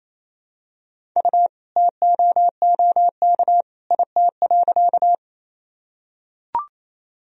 Morse code, heard as a keyed tone, about 28 wpm, 700 Hz. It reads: U TOOK IT.
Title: Sentences from Top 300 Words and Abbreviations 28wpm